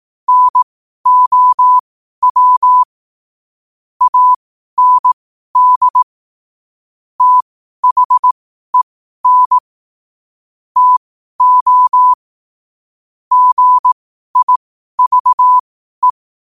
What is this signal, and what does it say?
Morse code, heard as a keyed tone, about 18 wpm, spaced as for 12 wpm, 1000 Hz. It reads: NOW AND THEN TO GIVE
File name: qotdfiles / 12wpm